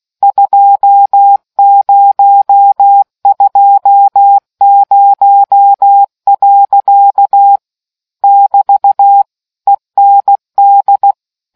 Morse code, heard as a keyed tone, about 16 wpm, 800 Hz. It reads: 2020. <BT> END